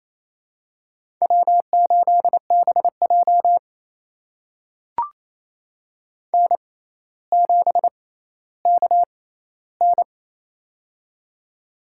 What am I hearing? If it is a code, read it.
Morse code, heard as a keyed tone, about 28 wpm, spaced as for 14 wpm, 700 Hz. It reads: W8BJ N7KN
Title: USA Call Signs - 4 Characters - 6x Character Spacing - 28wpm